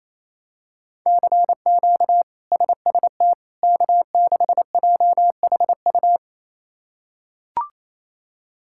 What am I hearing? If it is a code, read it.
Morse code, heard as a keyed tone, about 27 wpm, 700 Hz. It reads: CQ SST K6JHU
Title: K1USN SST QSO Practice - 28wpm